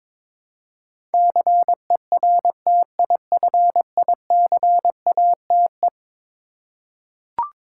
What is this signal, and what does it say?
Morse code, heard as a keyed tone, about 22 wpm, 700 Hz. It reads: CERTIFICATE
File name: practice / 22wpm / Eleven Letter Words - 22wpm